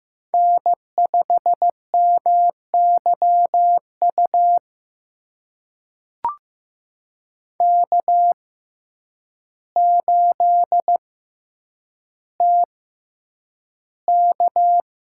Morse code, heard as a keyed tone, about 15 wpm, spaced as for 8 wpm, 700 Hz. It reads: N5MYU K8TK
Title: USA Call Signs - 5 Characters - 6x Character Spacing - 15wpm